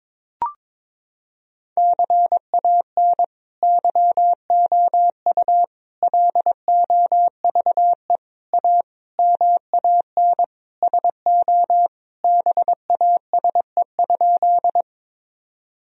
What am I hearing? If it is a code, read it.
Morse code, heard as a keyed tone, about 22 wpm, 700 Hz. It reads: CAN YOU LOVE A MAN SO BASE?
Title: Sentences from Top 1000 Words - Review Part 1 - 22wpm